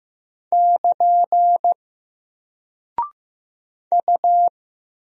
Morse code, heard as a keyed tone, about 15 wpm, 700 Hz. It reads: <KN> U
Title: Single Character (Letter:Number:Punctuation:Prosign) Alt 15wpm